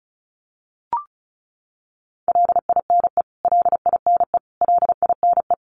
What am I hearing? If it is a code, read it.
Morse code, heard as a keyed tone, about 34 wpm, 700 Hz. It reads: LINE LINE LINE
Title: Top 200 Words Repeated 35wpm